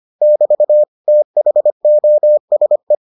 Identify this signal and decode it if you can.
Morse code, heard as a keyed tone, about 25 wpm, 600 Hz. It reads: <BT> THOSE